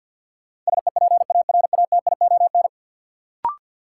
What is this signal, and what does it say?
Morse code, heard as a keyed tone, about 52 wpm, 700 Hz. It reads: SEPARATION
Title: Sets of 5 Words - New Word Drill - 50wpm